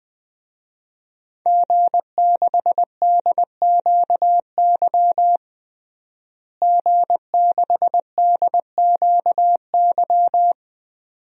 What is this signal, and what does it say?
Morse code, heard as a keyed tone, about 20 wpm, 700 Hz. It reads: G6DQY G6DQY
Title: DXCC-100 Call Signs - 5 Characters - Repeated 2x - 20wpm